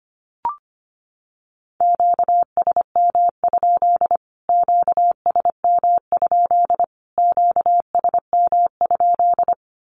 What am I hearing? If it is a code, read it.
Morse code, heard as a keyed tone, about 25 wpm, 700 Hz. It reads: QSM? QSM? QSM?
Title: Q-Codes Repeated 3x 25wpm